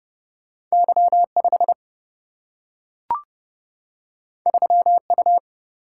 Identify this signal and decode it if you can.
Morse code, heard as a keyed tone, about 30 wpm, 700 Hz. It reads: Y 5 3 U
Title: Single-Letter Single-Number 30wpm